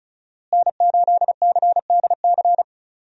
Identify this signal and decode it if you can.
Morse code, heard as a keyed tone, about 36 wpm, 700 Hz. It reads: N8CDC
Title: USA Call Signs - 5 Characters - Repeated 2x - 35wpm